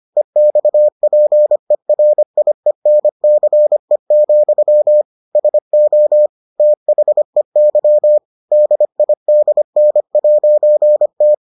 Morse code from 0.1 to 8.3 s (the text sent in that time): EXPERIENCE, SO THEY